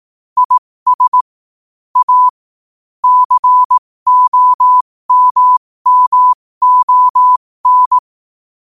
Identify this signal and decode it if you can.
Morse code, heard as a keyed tone, about 18 wpm, spaced as for 15 wpm, 1000 Hz. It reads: IS A COMMON
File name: qotdfiles / 15wpm